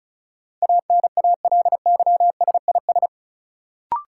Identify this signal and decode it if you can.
Morse code, heard as a keyed tone, about 34 wpm, 700 Hz. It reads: ANALYSIS
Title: Sentences from Top 1400 Words - New Word Drill Alt - 35wpm